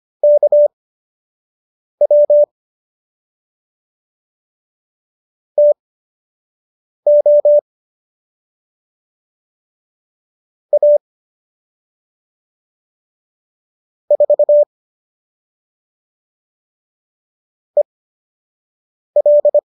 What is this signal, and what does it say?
Morse code, heard as a keyed tone, about 25 wpm, spaced as for 6 wpm, 590 Hz. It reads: KW TO A 4 EL
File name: qso003 06